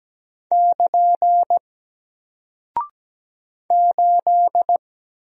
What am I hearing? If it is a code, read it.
Morse code, heard as a keyed tone, about 17 wpm, 700 Hz. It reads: <KN> 8
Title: Single Character (Letter:Number:Punctuation:Prosign) Alt 17wpm